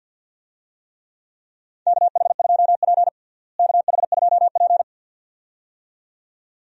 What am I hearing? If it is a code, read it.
Morse code, heard as a keyed tone, about 50 wpm, 700 Hz. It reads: KH2P KH2P
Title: USA Call Signs - 4 Characters - Repeated 2x - 50wpm